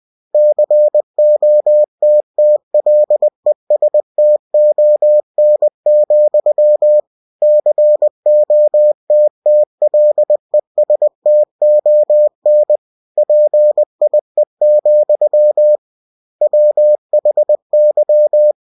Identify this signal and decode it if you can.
Morse code, heard as a keyed tone, about 20 wpm, 600 Hz. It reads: COTTLESTON, COTTLESTON PIE, WHY
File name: cw-books / winnie-the-pooh / winnie-20wpm-page-0046